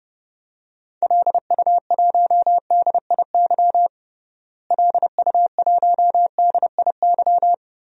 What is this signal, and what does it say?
Morse code, heard as a keyed tone, about 32 wpm, 700 Hz. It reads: LU1DIY LU1DIY
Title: DXCC-100 Call Signs - 6 Characters - Repeated 3x - 30wpm